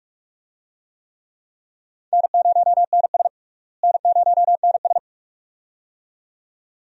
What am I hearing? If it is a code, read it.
Morse code, heard as a keyed tone, about 47 wpm, 700 Hz. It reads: N0NS N0NS
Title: USA Call Signs - 4 Characters - Repeated 2x - 45wpm